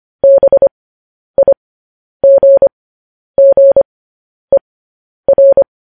Morse code, heard as a keyed tone, about 25 wpm, spaced as for 10 wpm, 550 Hz. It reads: BIGGER